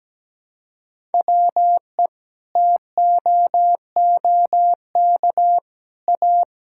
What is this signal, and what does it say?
Morse code, heard as a keyed tone, about 17 wpm, 700 Hz. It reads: WE TOOK A
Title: Sentences from Top 1000 Words - Review Part 6 - 17wpm